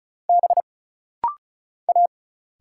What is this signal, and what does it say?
Morse code, heard as a keyed tone, about 34 wpm, 700 Hz. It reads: B A
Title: Single Letter-Number - Rapid-Fire 35wpm